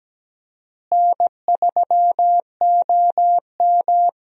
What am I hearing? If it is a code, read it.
Morse code, heard as a keyed tone, about 17 wpm, 700 Hz. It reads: N3OM
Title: USA Call Signs - 4 Characters - Repeated 3x - 17wpm